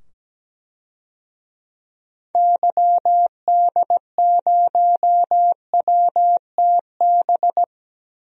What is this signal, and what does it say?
Morse code, heard as a keyed tone, about 17 wpm, 700 Hz. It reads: YD0WTB
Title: DXCC-100 Call Signs - 6 Characters - 17wpm